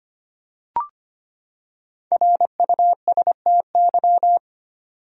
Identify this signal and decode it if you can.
Morse code, heard as a keyed tone, about 25 wpm, 700 Hz. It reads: RUSTY
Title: USA Common Names - Max 5 Letters - 25wpm